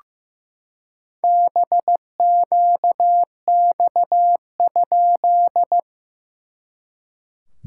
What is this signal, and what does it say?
Morse code, heard as a keyed tone, about 16 wpm, 700 Hz. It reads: BQX?
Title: SOTA QSO Practice 15wpm